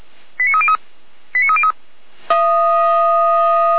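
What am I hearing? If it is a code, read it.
Five-tone selective call (ZVEI), heard as quick runs of five tones, tones of about 70 ms. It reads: 89303 89303
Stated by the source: ZVEI five-tone call received on an SDR, from the Feuerwehr Orlamünde fire brigade audio server.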